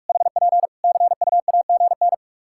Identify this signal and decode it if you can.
Morse code, heard as a keyed tone, about 46 wpm, 700 Hz. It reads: HP CUAGN
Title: Practice QSO - Exchange 4 - Ending - 45wpm